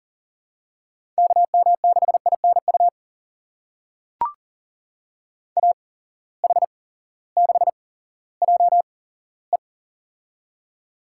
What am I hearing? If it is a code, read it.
Morse code, heard as a keyed tone, about 40 wpm, spaced as for 18 wpm, 690 Hz. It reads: KM6INU AH6JE
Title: USA Call Signs - 8x Character Spacing - 40wpm